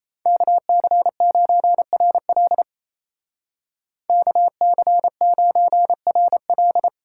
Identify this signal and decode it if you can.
Morse code, heard as a keyed tone, about 31 wpm, 700 Hz. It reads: KC9RL KC9RL
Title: Speed Racing - USA 2x2 Call Signs 28wpm